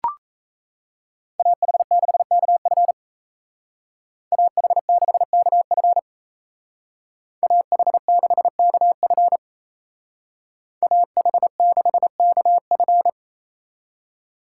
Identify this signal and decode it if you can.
Morse code, heard as a keyed tone, about 36 wpm, 700 Hz. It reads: AH6KF AH6KF AH6KF AH6KF